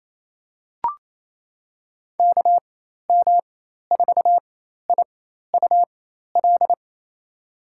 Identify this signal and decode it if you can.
Morse code, heard as a keyed tone, about 28 wpm, spaced as for 13 wpm, 700 Hz. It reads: KM4IUL
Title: USA Call Signs - 6 Characters - 4x Character Spacing - 28wpm